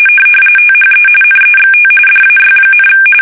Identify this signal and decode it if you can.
RTTY, about 45.45 baud. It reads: THE NEXT DAY,
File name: fsk850 / USS-Massachusetts